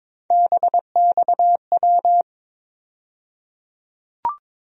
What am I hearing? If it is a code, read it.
Morse code, heard as a keyed tone, about 22 wpm, 700 Hz. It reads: BXW
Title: USA Call Sign Postfixes - 3 Letters 22wpm